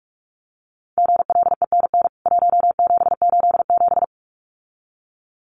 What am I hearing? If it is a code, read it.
Morse code, heard as a keyed tone, about 44 wpm, 700 Hz. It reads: GLENN 1787
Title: CWT Exchange 45wpm